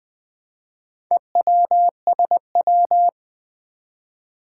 Morse code, heard as a keyed tone, about 20 wpm, 700 Hz. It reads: EWSW
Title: Four Challenging Letter-Number Combinations 20wpm